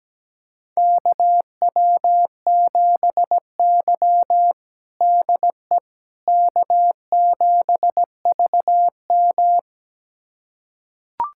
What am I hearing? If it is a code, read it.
Morse code, heard as a keyed tone, about 17 wpm, 700 Hz. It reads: KW7Y DE K7VM
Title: USA Callsign DE Callsign - 4 Characters - 17wpm